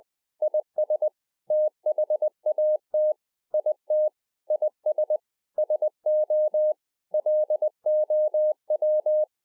Morse code, heard as a keyed tone, about 20 wpm, 610 Hz. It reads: IS THAT IT IS SO LOW
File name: 2011-04-mp3 / A SMALL TRANCEIVER 3 20WPM